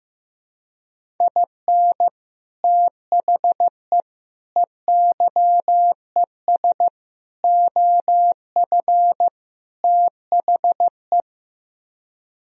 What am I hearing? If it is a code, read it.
Morse code, heard as a keyed tone, about 15 wpm, 700 Hz. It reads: IN THE EYES OF THE
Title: Speed-Racing - Sets of 5 Words 15wpm